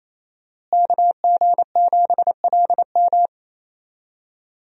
Morse code, heard as a keyed tone, about 28 wpm, 700 Hz. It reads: KG7LM
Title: USA Call Signs - 5 Characters - Repeated 2x - 28wpm